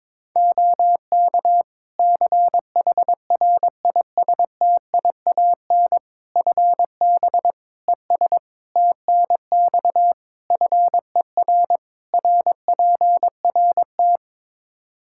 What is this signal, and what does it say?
Morse code, heard as a keyed tone, about 21 wpm, 700 Hz. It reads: OK CHRISTIAN FB ES TNX FER RPRT
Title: Practice QSO - Exchange 2 - Info - 22wpm